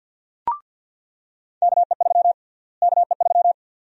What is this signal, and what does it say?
Morse code, heard as a keyed tone, about 50 wpm, 700 Hz. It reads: KE3 KE3
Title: USA Call Sign Prefixes - Repeated 2x - 50wpm